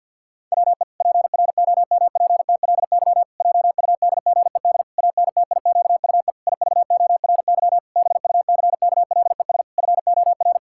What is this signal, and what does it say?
Morse code, heard as a keyed tone, about 50 wpm, 700 Hz. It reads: WE PROMPTLY JUDGED ANTIQUE IVORY BUCKLES FOR
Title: Pangram Challenge v2 - 50wpm